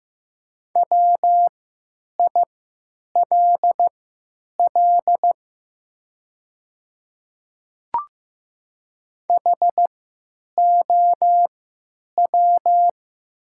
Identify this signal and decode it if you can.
Morse code, heard as a keyed tone, about 15 wpm, spaced as for 9 wpm, 690 Hz. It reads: WILL HOW
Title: 3x Character Spacing - Top 100 Words - 15wpm